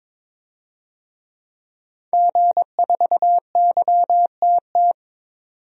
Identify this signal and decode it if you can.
Morse code, heard as a keyed tone, about 22 wpm, 700 Hz. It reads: G4YTT
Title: DXCC-100 Call Signs - 5 Characters - 22wpm